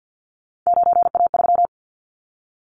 Morse code, heard as a keyed tone, about 50 wpm, 700 Hz. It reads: 9A3
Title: Callsign Prefixes for Top 100 DXCC - 50wpm